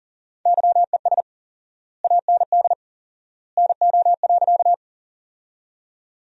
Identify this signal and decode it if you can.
Morse code, heard as a keyed tone, about 40 wpm, 700 Hz. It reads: YES AND NO.
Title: 4x Word-Spacing - Sentences from Top 500 Words - 40wpm